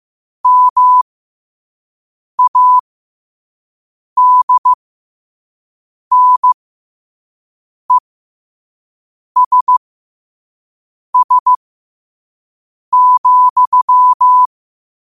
Morse code, heard as a keyed tone, about 15 wpm, spaced as for 5 wpm, 1000 Hz. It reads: MADNESS,
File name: qotdfiles / 5wpm